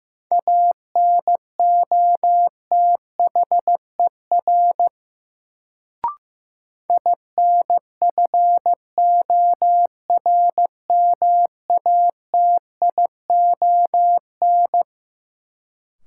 Morse code, heard as a keyed tone, about 15 wpm, 700 Hz. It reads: ANOTHER INFORMATION
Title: Sentences from Top 600 Words - New Word Drill - 15wpm